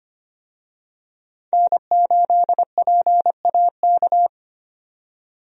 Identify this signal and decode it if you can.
Morse code, heard as a keyed tone, about 25 wpm, 700 Hz. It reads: N8PAK
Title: USA Call Signs - 5 Characters - 25wpm